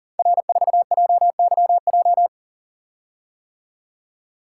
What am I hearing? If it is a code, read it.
Morse code, heard as a keyed tone, about 40 wpm, 700 Hz. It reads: R4JYJ
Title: Code Groups - 5 Letter:Numbers - 40wpm